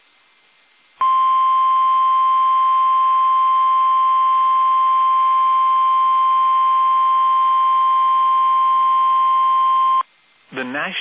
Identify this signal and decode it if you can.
Emergency Alert System (SAME) tones (the 1050 Hz attention tone), with speech. It is heard from 1.0 to 10.1 s.